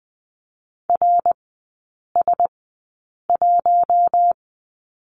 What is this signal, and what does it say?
Morse code, heard as a keyed tone, about 20 wpm, spaced as for 8 wpm, 700 Hz. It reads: RS1
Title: 107 Lesson 18e - Letter m - Three Characters 20wpm